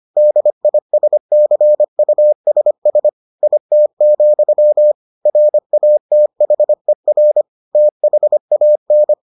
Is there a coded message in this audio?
Yes: Morse code from 0.2 to 9.1 s, heard as a keyed tone, about 25 wpm, 600 Hz. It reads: DISCUSS IT, RATHER THAN